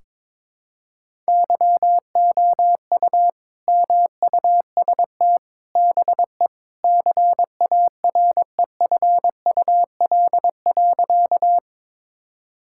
Morse code, heard as a keyed tone, about 22 wpm, 700 Hz. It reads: YOU MUST BE CAREFUL.
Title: Sentences from Top 1600 Words 22wpm